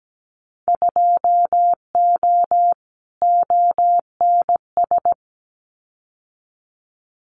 Morse code, heard as a keyed tone, about 17 wpm, 700 Hz. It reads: 2O ONS
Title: Winter Field Day Exchanges Encore 17wpm